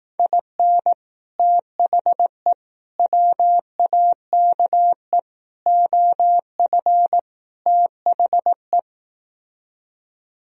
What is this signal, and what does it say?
Morse code, heard as a keyed tone, about 18 wpm, 700 Hz. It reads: IN THE WAKE OF THE
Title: Speed-Racing - Sets of 5 Words 15wpm